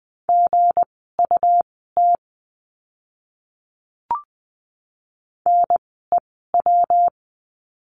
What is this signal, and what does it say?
Morse code, heard as a keyed tone, about 20 wpm, spaced as for 15 wpm, 700 Hz. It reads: GUT NEW
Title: 160 Lesson 26f - Letter g - Words 20wpm